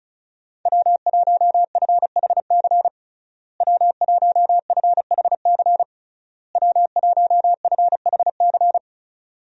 Morse code, heard as a keyed tone, about 36 wpm, 700 Hz. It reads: W1FHC W1FHC W1FHC
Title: USA Call Signs - 5 Characters - Repeated 3x - 35wpm